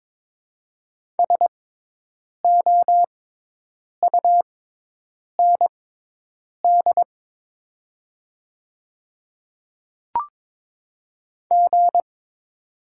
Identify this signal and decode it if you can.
Morse code, heard as a keyed tone, about 23 wpm, spaced as for 8 wpm, 700 Hz. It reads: SOUND G
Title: 6x Character Spacing - Top 200 Words - 22wpm